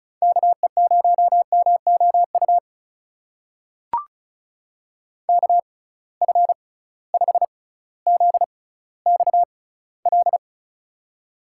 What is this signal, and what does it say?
Morse code, heard as a keyed tone, about 36 wpm, spaced as for 18 wpm, 700 Hz. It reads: KE0MOU KF5ZXL
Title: USA Call Signs - 6 Characters - 6x Character Spacing - 35wpm